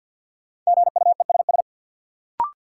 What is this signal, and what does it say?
Morse code, heard as a keyed tone, about 51 wpm, 700 Hz. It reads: GUESS
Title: Sentences from Top 1500 Words - New Word Drill Alt - 50wpm